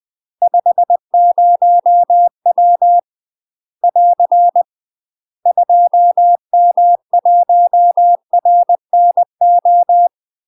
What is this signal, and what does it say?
Morse code, heard as a keyed tone, about 20 wpm, 700 Hz. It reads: 50W <AR> 2M1RNO
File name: UK MASTER QSOs TRAINING/20WPM